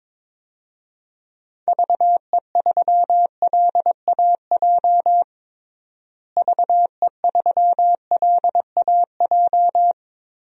Morse code, heard as a keyed tone, about 22 wpm, 700 Hz. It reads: VE3LAJ VE3LAJ
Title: DXCC-100 Call Signs - 6 Characters - Repeated 2x - 22wpm